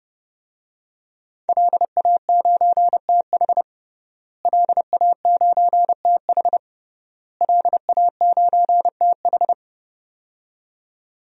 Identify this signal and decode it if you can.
Morse code, heard as a keyed tone, about 30 wpm, 700 Hz. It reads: LA9TH LA9TH LA9TH
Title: DXCC-100 Call Signs - 5 Characters - Repeated 3x - 30wpm